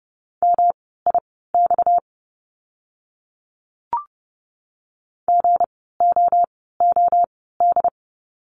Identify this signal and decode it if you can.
Morse code, heard as a keyed tone, about 30 wpm, spaced as for 17 wpm, 700 Hz. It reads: MIX GOOD
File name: course / 30:10wpm / 230 Lesson 37f - Letter x - Words 30wpm